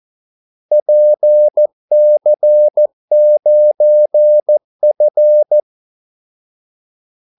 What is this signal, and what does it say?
Morse code, heard as a keyed tone, about 14 wpm, 600 Hz. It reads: PC9F
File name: Callsigns-14wpm